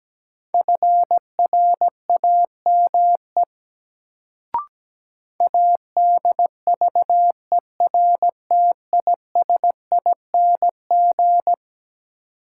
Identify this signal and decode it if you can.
Morse code, heard as a keyed tone, about 17 wpm, 700 Hz. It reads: FRAME ADVERTISING
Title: Sentences from Top 1000 Words - Review Part 6 - New Word Drill - 17wpm